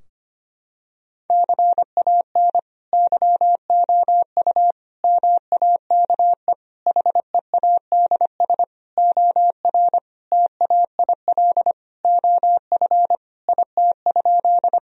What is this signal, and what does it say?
Morse code, heard as a keyed tone, about 25 wpm, 700 Hz. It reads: CAN YOU MAKE HEADS OR TAIL OF IT?
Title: Sentences from Top 1000 Words - Review Part 3 - 25wpm